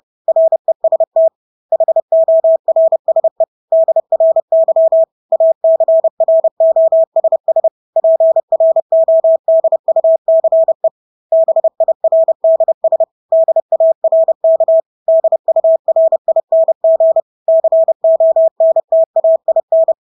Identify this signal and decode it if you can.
Morse code, heard as a keyed tone, about 30 wpm, 650 Hz. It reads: REST HORSE DRY ACROSS PRODUCE BIRDS DARK DURING CONTAIN
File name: Engelske ord 30-30 wpm